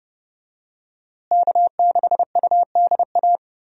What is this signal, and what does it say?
Morse code, heard as a keyed tone, about 30 wpm, 700 Hz. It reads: K6UDA
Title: USA Call Signs - 5 Characters - Repeated 2x - 30wpm